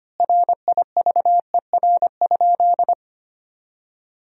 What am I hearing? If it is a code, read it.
Morse code, heard as a keyed tone, about 25 wpm, 700 Hz. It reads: RIVER?
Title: 3x Word-Spacing - Sentences from Top 300 Words - 25wpm